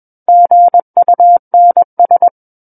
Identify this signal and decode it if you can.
Morse code, heard as a keyed tone, about 21 wpm, 700 Hz. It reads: GUNS
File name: Book One / 12-Chapter 12 - 21 WPM